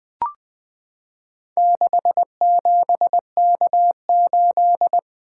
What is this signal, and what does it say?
Morse code, heard as a keyed tone, about 20 wpm, 700 Hz. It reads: 67K8
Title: Four Letter-Number Combinations - 20wpm